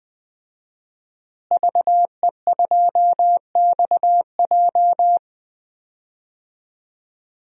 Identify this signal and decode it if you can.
Morse code, heard as a keyed tone, about 21 wpm, 700 Hz. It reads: VE2XJ
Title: DXCC-100 Call Signs - 5 Characters - Repeated 3x - 20wpm